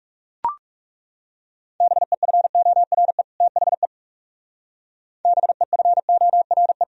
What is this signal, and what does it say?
Morse code, heard as a keyed tone, about 40 wpm, 700 Hz. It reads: BEFORE THE BEFORE